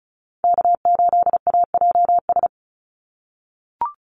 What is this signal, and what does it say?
Morse code, heard as a keyed tone, about 36 wpm, 700 Hz. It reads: K8AJS
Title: Call Signs - 35wpm